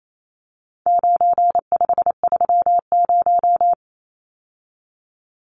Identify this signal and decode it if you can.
Morse code, heard as a keyed tone, about 28 wpm, 700 Hz. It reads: 9530
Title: Numbers - 4-Digit 28wpm